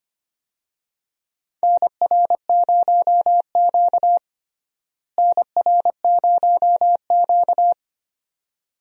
Morse code, heard as a keyed tone, about 25 wpm, 700 Hz. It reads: NR0Q NR0Q
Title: USA Call Signs - 4 Characters - Repeated 2x - 25wpm